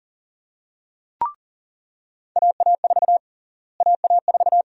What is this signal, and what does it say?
Morse code, heard as a keyed tone, about 39 wpm, 700 Hz. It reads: AA4 AA4
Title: USA Call Sign Prefixes - Repeated 2x - 40wpm